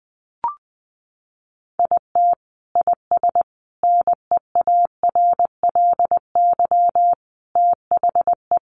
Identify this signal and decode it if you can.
Morse code, heard as a keyed tone, about 20 wpm, 700 Hz. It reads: IT IS NEARLY THE
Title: Sentences from Top 1300 Words 20wpm